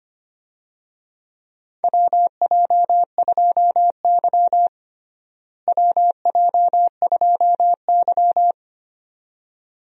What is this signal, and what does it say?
Morse code, heard as a keyed tone, about 25 wpm, 700 Hz. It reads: WJ2Y WJ2Y
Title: USA Call Signs - 4 Characters - Repeated 2x - 25wpm